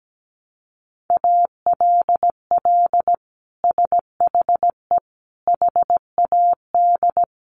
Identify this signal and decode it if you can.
Morse code, heard as a keyed tone, about 17 wpm, 700 Hz. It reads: ALL SHE HAD
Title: Sentences from Top 1100 Words 17wpm